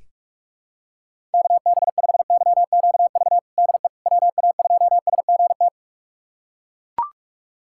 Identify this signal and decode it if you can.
Morse code, heard as a keyed tone, about 41 wpm, 700 Hz. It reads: KB5YQV DE WA2SGT
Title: USA Callsign DE Callsign 45wpm